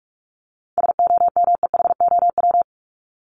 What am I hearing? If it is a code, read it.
Morse code, heard as a keyed tone, about 44 wpm, 700 Hz. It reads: SOMEHOW
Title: Top 1501-1750 Words 45wpm